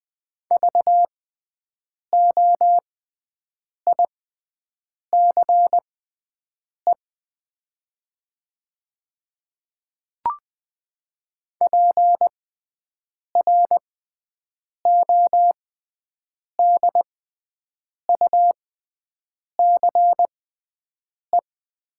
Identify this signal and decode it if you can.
Morse code, heard as a keyed tone, about 20 wpm, spaced as for 7 wpm, 700 Hz. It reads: VOICE PRODUCE
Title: 6x Character Spacing - Top 500 Words - 20wpm